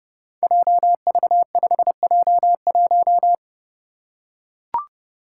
Morse code, heard as a keyed tone, about 30 wpm, 700 Hz. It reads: JV5J1
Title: Five Challenging Letter-Number Combinations 30wpm